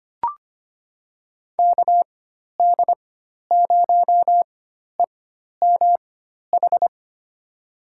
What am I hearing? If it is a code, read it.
Morse code, heard as a keyed tone, about 25 wpm, spaced as for 12 wpm, 700 Hz. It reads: KD0EMH
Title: USA Call Signs - 6 Characters - 4x Character Spacing - 25wpm